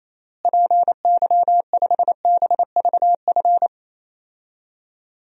Morse code, heard as a keyed tone, about 28 wpm, 700 Hz. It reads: PY5BVF
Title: DXCC-100 Call Signs - 6 Characters - Repeated 3x - 28wpm